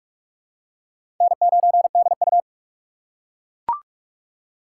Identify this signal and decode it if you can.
Morse code, heard as a keyed tone, about 41 wpm, 700 Hz. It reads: N9DU
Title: USA Call Signs - 4 Characters - 3x Character Spacing - 45wpm